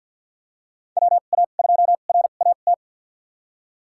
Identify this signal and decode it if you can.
Morse code, heard as a keyed tone, about 52 wpm, spaced as for 36 wpm, 700 Hz. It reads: WA2RAT
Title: USA Call Signs - 2x Character Spacing - 50wpm